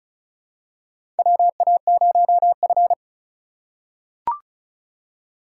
Morse code, heard as a keyed tone, about 35 wpm, 690 Hz. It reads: WA0F
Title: USA Call Signs - 3x Character Spacing - 35wpm